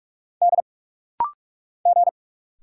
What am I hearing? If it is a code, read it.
Morse code, heard as a keyed tone, about 46 wpm, 700 Hz. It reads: D G
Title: Single Letter-Number Alt - Rapid-Fire 45wpm